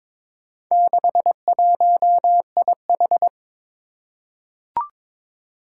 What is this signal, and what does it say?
Morse code, heard as a keyed tone, about 22 wpm, 700 Hz. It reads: 61IH
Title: Four Challenging Letter-Number Combinations 22wpm